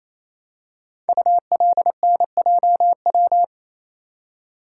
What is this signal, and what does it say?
Morse code, heard as a keyed tone, about 27 wpm, 700 Hz. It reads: ULNJW
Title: Code Groups - 5 Letters - 28wpm